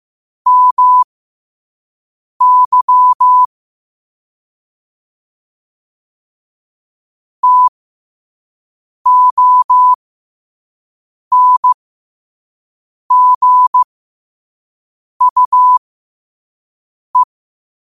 Morse code, heard as a keyed tone, about 15 wpm, spaced as for 5 wpm, 1000 Hz. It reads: MY TONGUE
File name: qotdfiles / 5wpm